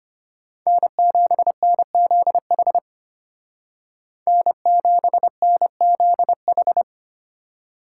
Morse code, heard as a keyed tone, about 33 wpm, 700 Hz. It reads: N7NZH N7NZH